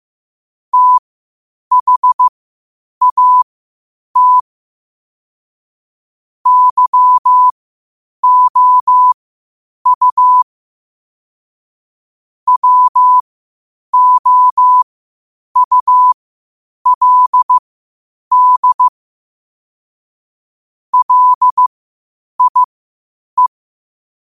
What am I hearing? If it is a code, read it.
Morse code, heard as a keyed tone, about 15 wpm, spaced as for 8 wpm, 1000 Hz. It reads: THAT YOU WOULD LIE
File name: qotdfiles / 8wpm